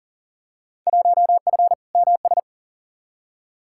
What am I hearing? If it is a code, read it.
Morse code, heard as a keyed tone, about 40 wpm, 700 Hz. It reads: 1F MS
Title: Field Day Exchanges Encore - 40wpm